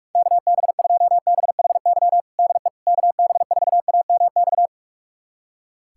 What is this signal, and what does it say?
Morse code, heard as a keyed tone, about 41 wpm, 700 Hz. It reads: KB2BHY DE KB4AMX
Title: USA Callsign DE Callsign - 6 Characters - 45wpm